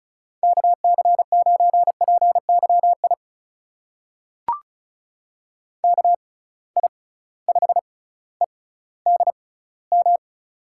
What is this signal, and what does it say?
Morse code, heard as a keyed tone, about 36 wpm, spaced as for 18 wpm, 700 Hz. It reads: KC9PYI KI5EDM
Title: USA Call Signs - 6 Characters - 6x Character Spacing - 35wpm